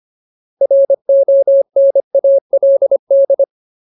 Morse code, heard as a keyed tone, about 25 wpm, 550 Hz. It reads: RONALD
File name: Ultimate Word List / 25WPM Ultimate